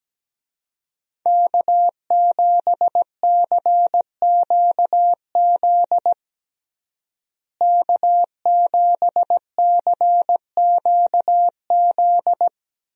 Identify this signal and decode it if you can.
Morse code, heard as a keyed tone, about 17 wpm, 700 Hz. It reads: K7CQZ K7CQZ
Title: USA Call Signs - 5 Characters - Repeated 3x - 17wpm